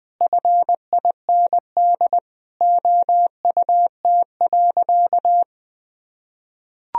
Morse code, heard as a keyed tone, about 20 wpm, 700 Hz. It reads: FIND OUT.
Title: Sentences from Top 200 Words and Abbreviations 20wpm